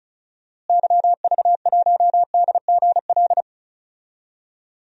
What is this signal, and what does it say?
Morse code, heard as a keyed tone, about 36 wpm, 700 Hz. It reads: YV1DGL
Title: DXCC-100 Call Signs - 6 Characters - Repeated 3x - 35wpm